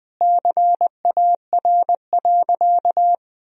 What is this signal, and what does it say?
Morse code, heard as a keyed tone, about 20 wpm, 700 Hz. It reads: CAR.
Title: Sentences from Top 1000 Words - Review Part 1 - 20wpm